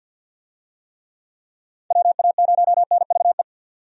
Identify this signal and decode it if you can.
Morse code, heard as a keyed tone, about 48 wpm, 700 Hz. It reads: WA0NVE
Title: USA Call Signs - 6 Characters - 50wpm